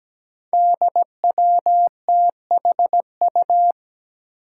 Morse code, heard as a keyed tone, about 17 wpm, 700 Hz. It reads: DWTHU
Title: Five Challenging Letter-Number Combinations 17wpm